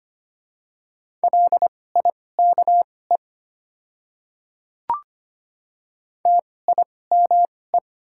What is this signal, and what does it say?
Morse code, heard as a keyed tone, about 25 wpm, spaced as for 18 wpm, 700 Hz. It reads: LIKE TIME
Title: 2x Character Spacing - Top 100 Words - 25wpm